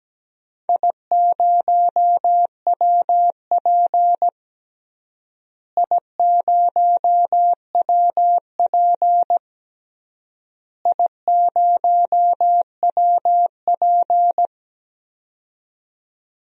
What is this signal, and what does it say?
Morse code, heard as a keyed tone, about 17 wpm, 700 Hz. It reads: I0WP I0WP I0WP
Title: DXCC-100 Call Signs - 4 Characters - Repeated 3x - 17wpm